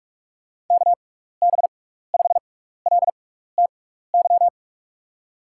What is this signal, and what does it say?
Morse code, heard as a keyed tone, about 44 wpm, spaced as for 15 wpm, 700 Hz. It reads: KB5LTY
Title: USA Call Signs - 6 Characters - 6x Character Spacing - 45wpm